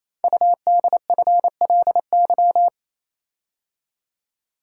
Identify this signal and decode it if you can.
Morse code, heard as a keyed tone, about 27 wpm, 700 Hz. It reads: UDFLY
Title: Code Groups - 5 Letters - 28wpm